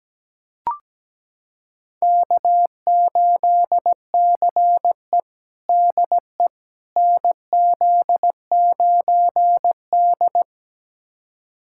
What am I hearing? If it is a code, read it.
Morse code, heard as a keyed tone, about 17 wpm, 700 Hz. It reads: K8CE DE NZ9D
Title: USA Callsign DE Callsign - 4 Characters - 17wpm